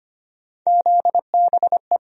Morse code, heard as a keyed tone, about 25 wpm, 700 Hz. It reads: ZBE
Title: USA Call Sign Postfixes - 3 Letters 25wpm